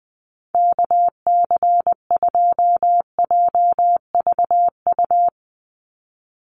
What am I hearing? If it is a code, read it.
Morse code, heard as a keyed tone, about 20 wpm, 700 Hz. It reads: KC2JVU
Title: USA Call Signs 20wpm